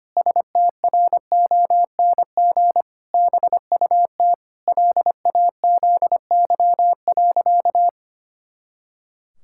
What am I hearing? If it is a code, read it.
Morse code, heard as a keyed tone, about 25 wpm, 700 Hz. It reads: STRONG BUT LAZY.
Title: Pangram Challenge v2 - 25wpm